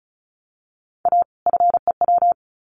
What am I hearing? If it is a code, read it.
Morse code, heard as a keyed tone, about 34 wpm, 700 Hz. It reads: A FEW
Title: Sets of 2 Words 35wpm